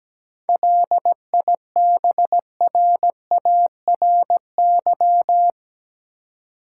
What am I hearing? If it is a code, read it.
Morse code, heard as a keyed tone, about 17 wpm, 700 Hz. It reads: LIBRARY
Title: Seven Letter Words - 17wpm